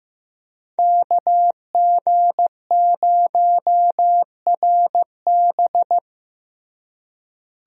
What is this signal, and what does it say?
Morse code, heard as a keyed tone, about 16 wpm, 700 Hz. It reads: KG0RB
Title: USA Call Signs - 5 Characters - Repeated 3x - 15wpm